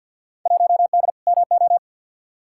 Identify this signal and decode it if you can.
Morse code, heard as a keyed tone, about 51 wpm, 700 Hz. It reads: 1D MO
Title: Field Day Exchanges - 50wpm